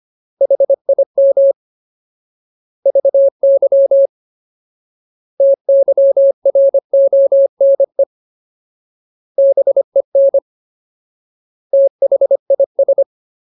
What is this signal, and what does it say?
Morse code, heard as a keyed tone, about 25 wpm, 550 Hz. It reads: HIM VY TYRONE BEN THIS
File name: Ultimate Word List / 25WPM Ultimate